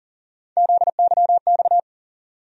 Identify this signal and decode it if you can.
Morse code, heard as a keyed tone, about 40 wpm, 700 Hz. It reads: ZYX
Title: USA Call Sign Postfixes - 3 Letters 40wpm